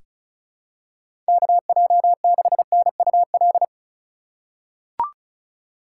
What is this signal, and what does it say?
Morse code, heard as a keyed tone, about 36 wpm, 700 Hz. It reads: KJ6NUL
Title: USA Call Signs - 6 Characters - 2x Character Spacing - 35wpm